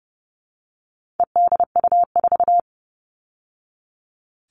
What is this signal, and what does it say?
Morse code, heard as a keyed tone, about 30 wpm, 700 Hz. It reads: EDU4
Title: All Cut Numbers - 4-Digit 30wpm